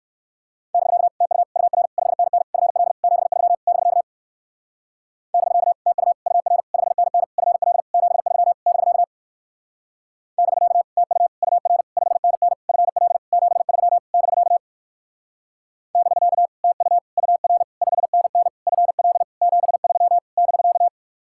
Morse code, heard as a keyed tone, about 57 wpm, 700 Hz. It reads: <BK> TU UR 5NN FL 73 <BK> <BK> TU UR 5NN FL 73 <BK> <BK> TU UR 5NN FL 73 <BK> <BK> TU UR 5NN FL 73 <BK>